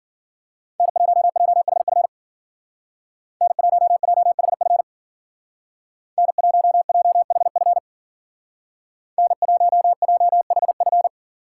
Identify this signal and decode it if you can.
Morse code, heard as a keyed tone, about 49 wpm, 700 Hz. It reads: N1JHF N1JHF N1JHF N1JHF
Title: Speed Racing - USA Call Signs 40wpm